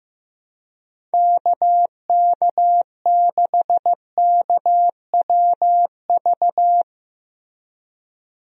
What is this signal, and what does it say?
Morse code, heard as a keyed tone, about 15 wpm, 700 Hz. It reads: KK6KWV
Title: USA Call Signs - 6 Characters - Repeated 3x - 15wpm